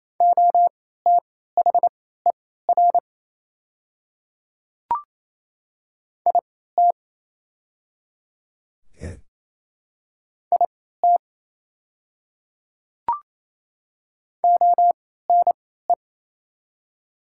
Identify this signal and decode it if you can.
Morse code, heard as a keyed tone, about 27 wpm, spaced as for 16 wpm, 700 Hz. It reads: OTHER IT IT ONE
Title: 3x Character Spacing - Top 100 Words - 28wpm